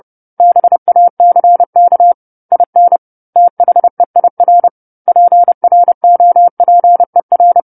Morse code, heard as a keyed tone, about 30 wpm, 700 Hz. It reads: BACK IN THEIR PROPER
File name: 10-Chapter-10 30 WPM